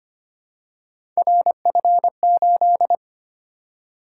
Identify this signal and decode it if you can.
Morse code, heard as a keyed tone, about 25 wpm, 700 Hz. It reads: RF8
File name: Three Letter-Number Combinations - 25wpm